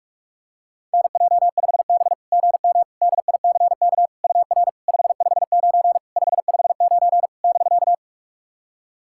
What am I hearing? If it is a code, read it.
Morse code, heard as a keyed tone, about 44 wpm, 690 Hz. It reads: NJ5B GM DICK UR 559 559 <BK>
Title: SOTA QSO Practice 45wpm